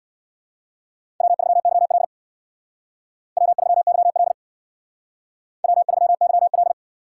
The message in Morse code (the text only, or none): W3YL W3YL W3YL